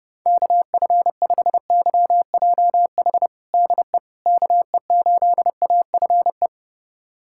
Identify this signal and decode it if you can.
Morse code, heard as a keyed tone, about 30 wpm, 700 Hz. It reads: KF5YJH DE KE8AFE
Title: USA Callsign DE Callsign - 6 Characters - 30wpm